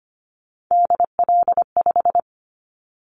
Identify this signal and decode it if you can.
Morse code, heard as a keyed tone, about 25 wpm, 700 Hz. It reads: DL5
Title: Callsign Prefixes for Top 50 DXCC - 25wpm